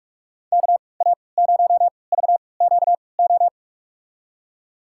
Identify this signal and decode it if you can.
Morse code, heard as a keyed tone, about 47 wpm, spaced as for 26 wpm, 700 Hz. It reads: KA0VQO
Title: USA Call Signs - 6 Characters - 3x Character Spacing - 45wpm